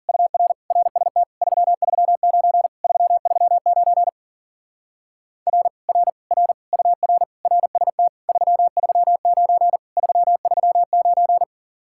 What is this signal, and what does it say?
Morse code, heard as a keyed tone, about 50 wpm, 700 Hz. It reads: UR RST 339 339 R R R UR RST 339 339